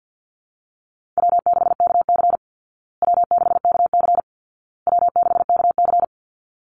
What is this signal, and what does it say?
Morse code, heard as a keyed tone, about 49 wpm, 700 Hz. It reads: W6KC W6KC W6KC
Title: Easy Call Signs Repeated 3x - 50wpm